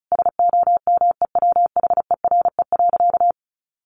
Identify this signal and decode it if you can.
Morse code, heard as a keyed tone, about 34 wpm, 700 Hz. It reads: SOMEWHERE.
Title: Sentences from Top 1400 Words 35wpm